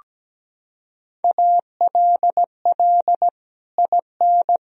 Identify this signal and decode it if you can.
Morse code, heard as a keyed tone, about 17 wpm, 700 Hz. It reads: ALL IN
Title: Sentences from Top 200 Words and Abbreviations 17wpm